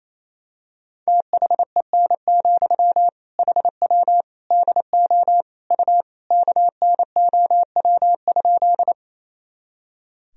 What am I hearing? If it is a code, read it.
Morse code, heard as a keyed tone, about 28 wpm, 700 Hz. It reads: THEN, HW DO U KNOW?
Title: Sentences from Top 100 Words and Abbreviations 28wpm